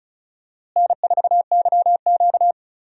Morse code, heard as a keyed tone, about 34 wpm, 700 Hz. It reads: N4YQ
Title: USA Call Signs - 4 Characters - Repeated 2x - 35wpm